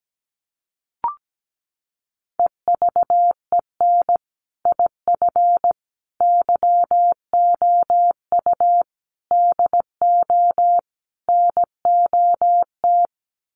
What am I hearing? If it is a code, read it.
Morse code, heard as a keyed tone, about 17 wpm, 700 Hz. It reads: EVEN IF YOU DO NOT
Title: Sets of 5 Words Encore 17wpm